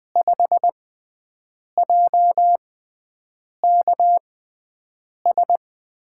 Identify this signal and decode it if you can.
Morse code, heard as a keyed tone, about 20 wpm, spaced as for 7 wpm, 700 Hz. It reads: 5JKS
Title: USA Call Signs - 6 Characters - 6x Character Spacing - 20wpm